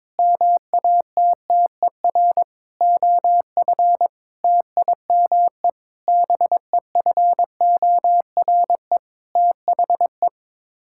Morse code, heard as a keyed tone, about 22 wpm, 700 Hz. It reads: MATTER OF TIME BEFORE THE
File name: Sets of 9 Words - 22wpm